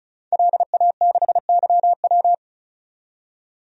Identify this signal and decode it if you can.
Morse code, heard as a keyed tone, about 36 wpm, 700 Hz. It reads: LA6YW
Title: DXCC-100 Call Signs - 5 Characters - 35wpm